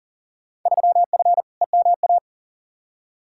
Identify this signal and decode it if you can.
Morse code, heard as a keyed tone, about 40 wpm, 700 Hz. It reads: 3F EMA
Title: Field Day Exchanges Encore - 40wpm